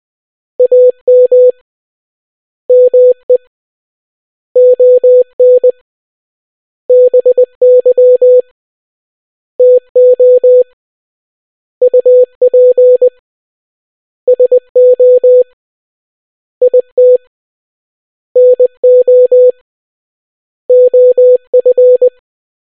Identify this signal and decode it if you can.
Morse code, heard as a keyed tone, about 20 wpm, 500 Hz. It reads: AM ME ON BY TO UP SO IT NO OF